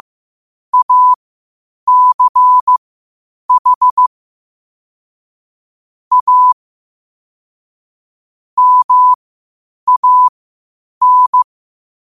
Morse code, heard as a keyed tone, about 15 wpm, spaced as for 8 wpm, 1000 Hz. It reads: ACH A MAN
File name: qotdfiles / 8wpm